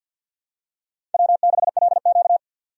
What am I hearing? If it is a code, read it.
Morse code, heard as a keyed tone, about 53 wpm, 700 Hz. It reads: W6LQ
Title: USA Call Signs - 4 Characters - 50wpm